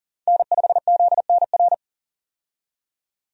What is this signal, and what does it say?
Morse code, heard as a keyed tone, about 40 wpm, 700 Hz. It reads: N5ZNR
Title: USA Call Signs - 5 Characters - 40wpm